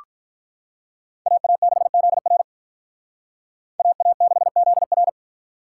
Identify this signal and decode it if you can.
Morse code, heard as a keyed tone, about 44 wpm, 700 Hz. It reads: AA6ZR AA6ZR
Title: Speed Racing - USA Call Signs 35wpm